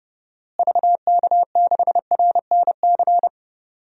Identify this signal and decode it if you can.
Morse code, heard as a keyed tone, about 30 wpm, 700 Hz. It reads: VK6RNC
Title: DXCC-100 Call Signs - 6 Characters - Repeated 3x - 30wpm